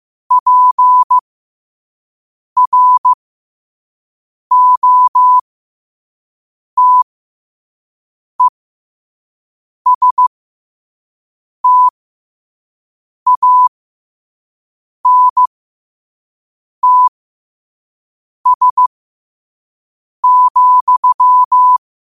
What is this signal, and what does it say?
Morse code, heard as a keyed tone, about 15 wpm, spaced as for 5 wpm, 1000 Hz. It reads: PROTESTANTS,
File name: qotdfiles / 5wpm